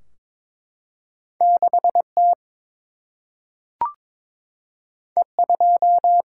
Morse code, heard as a keyed tone, about 22 wpm, 700 Hz. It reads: 6 T E 2
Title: Single-Letter Single-Number 22wpm